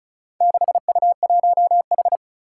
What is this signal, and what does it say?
Morse code, heard as a keyed tone, about 36 wpm, 700 Hz. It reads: 6U1H
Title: Four Challenging Letter-Number Combinations 35wpm